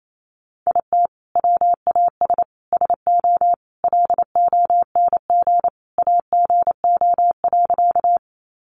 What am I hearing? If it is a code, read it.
Morse code, heard as a keyed tone, about 28 wpm, 700 Hz. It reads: IT WAS SO LONG AGO.
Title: Sentences from Top 500 Words 28wpm